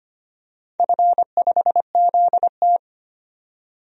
Morse code, heard as a keyed tone, about 25 wpm, 700 Hz. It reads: F5ZT
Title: DXCC-100 Call Signs - 4 Characters - 25wpm